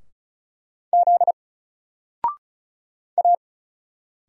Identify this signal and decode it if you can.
Morse code, heard as a keyed tone, about 36 wpm, 700 Hz. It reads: Z A
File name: Single-Letter-Number-35wpm